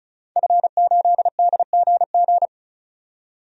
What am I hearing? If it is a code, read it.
Morse code, heard as a keyed tone, about 34 wpm, 700 Hz. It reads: F8DGG
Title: DXCC-100 Call Signs - 5 Characters - 35wpm